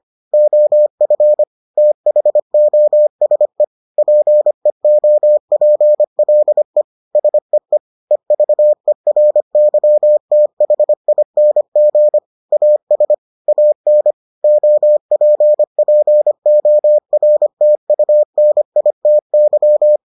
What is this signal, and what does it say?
Morse code, heard as a keyed tone, about 25 wpm, 600 Hz. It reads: OF THOSE PEOPLE SEE EVERYTHING AS AN OPPORTUNITY